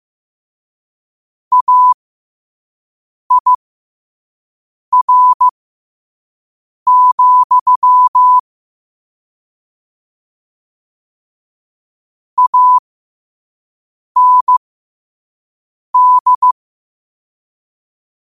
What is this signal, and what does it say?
Morse code, heard as a keyed tone, about 15 wpm, spaced as for 5 wpm, 1000 Hz. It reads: AIR, AND
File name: qotdfiles / 5wpm